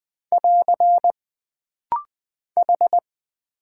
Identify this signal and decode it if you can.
Morse code, heard as a keyed tone, about 20 wpm, 700 Hz. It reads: <AR> H
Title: Single Character (Letter:Number:Punctuation:Prosign) - Rapid-Fire - 20wpm